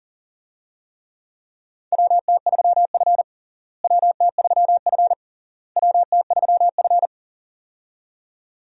Morse code, heard as a keyed tone, about 40 wpm, 700 Hz. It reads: WT3F WT3F WT3F
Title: USA Call Signs - 4 Characters - Repeated 3x - 40wpm